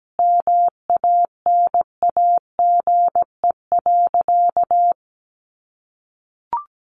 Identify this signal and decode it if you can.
Morse code, heard as a keyed tone, about 17 wpm, 700 Hz. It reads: MANAGE.
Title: Sentences from Top 1400 Words 17wpm